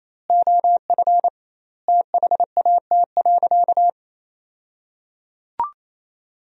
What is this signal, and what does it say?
Morse code, heard as a keyed tone, about 28 wpm, 700 Hz. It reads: OF THAT.
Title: 2x Word-Spacing - Sentences from Top 300 Words - 28wpm